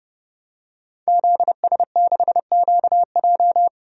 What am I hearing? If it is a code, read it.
Morse code, heard as a keyed tone, about 30 wpm, 700 Hz. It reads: ZS6QJ
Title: DXCC-100 Call Signs - 5 Characters - Repeated 2x - 30wpm